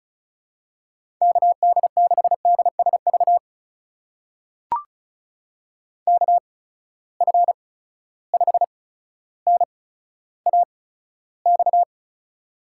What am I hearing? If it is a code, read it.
Morse code, heard as a keyed tone, about 36 wpm, spaced as for 15 wpm, 700 Hz. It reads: KD6DSV KF5NAX
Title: USA Call Signs - 6 Characters - 8x Character Spacing - 35wpm